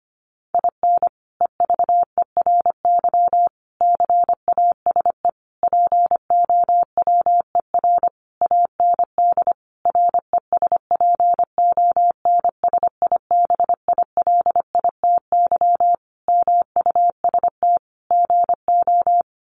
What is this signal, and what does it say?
Morse code, heard as a keyed tone, about 25 wpm, 700 Hz. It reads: IN EVERY CASE POWER AND RESPONSIBILITY MUST GO
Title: Sentences from Top 1200 Words 25wpm